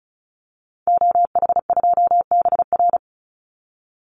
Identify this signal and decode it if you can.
Morse code, heard as a keyed tone, about 34 wpm, 700 Hz. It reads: OH2BR
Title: Call Signs - 35wpm